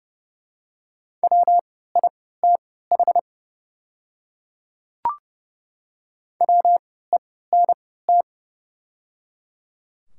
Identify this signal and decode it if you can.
Morse code, heard as a keyed tone, about 30 wpm, spaced as for 17 wpm, 700 Hz. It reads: WITH WENT
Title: 3x Character Spacing - Top 200 Words - 30wpm